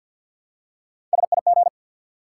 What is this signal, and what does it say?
Morse code, heard as a keyed tone, about 52 wpm, 700 Hz. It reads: SIG
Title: Top 90 QSO Elements 50wpm